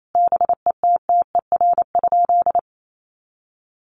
Morse code, heard as a keyed tone, about 28 wpm, 700 Hz. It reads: BETTER?
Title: Sentences from Top 1500 Words 28wpm